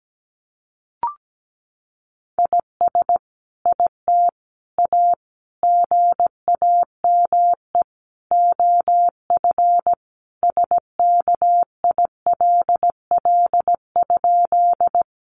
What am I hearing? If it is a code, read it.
Morse code, heard as a keyed tone, about 17 wpm, 700 Hz. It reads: IS IT A GAME OF SKILL?
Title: Sentences from Top 1200 Words 17wpm